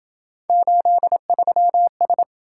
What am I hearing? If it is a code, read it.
Morse code, heard as a keyed tone, about 27 wpm, 700 Hz. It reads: 83S